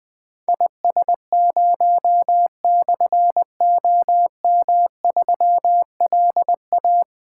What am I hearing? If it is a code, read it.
Morse code, heard as a keyed tone, about 20 wpm, 700 Hz. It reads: IS0/OM3LA
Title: Difficult Call Signs 20wpm